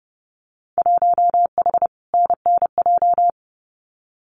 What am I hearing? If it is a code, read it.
Morse code, heard as a keyed tone, about 30 wpm, 700 Hz. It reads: 1H NNJ
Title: Winter Field Day Exchanges 30wpm